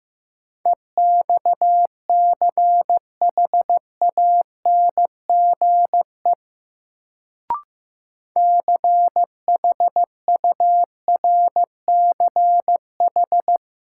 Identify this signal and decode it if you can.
Morse code, heard as a keyed tone, about 15 wpm, 700 Hz. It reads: EXCHANGE CHURCH
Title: Sets of 5 Words - New Word Drill - 15wpm